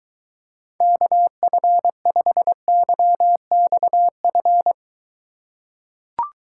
Morse code, heard as a keyed tone, about 22 wpm, 700 Hz. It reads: KF5YXF